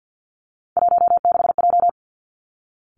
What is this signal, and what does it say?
Morse code, heard as a keyed tone, about 50 wpm, 700 Hz. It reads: 16J
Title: Three Challenging Letter-Number Combinations 50wpm